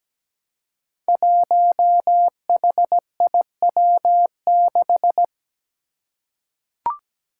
Morse code, heard as a keyed tone, about 17 wpm, 700 Hz. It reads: 1HIW6
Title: Five Challenging Letter-Number Combinations 17wpm